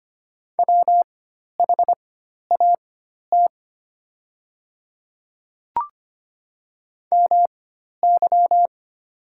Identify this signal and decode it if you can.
Morse code, heard as a keyed tone, about 25 wpm, spaced as for 12 wpm, 700 Hz. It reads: WHAT MY
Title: 4x Character Spacing - Top 100 Words - 25wpm